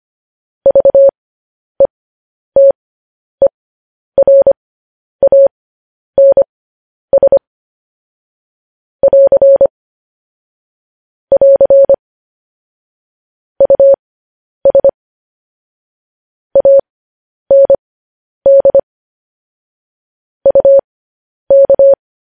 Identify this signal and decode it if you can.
Morse code, heard as a keyed tone, about 25 wpm, spaced as for 10 wpm, 550 Hz. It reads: VETERANS <AR> <AR> US AND UK
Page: News Headlines in Morse Code at 10 WPM